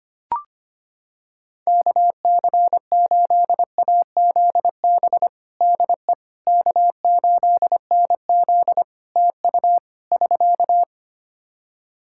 Morse code, heard as a keyed tone, about 25 wpm, 700 Hz. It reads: KC8AZB DE K8NZ TU <SK>
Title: Practice QSO - Exchange 4 - Ending - 25wpm